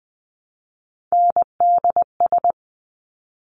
Morse code, heard as a keyed tone, about 20 wpm, 700 Hz. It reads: NDS
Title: Top Three Letter Combinations in Words 20wpm